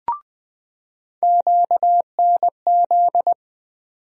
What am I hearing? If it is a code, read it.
Morse code, heard as a keyed tone, about 20 wpm, 700 Hz. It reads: QNZ
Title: USA Call Sign Postfixes - 3 Letters 20wpm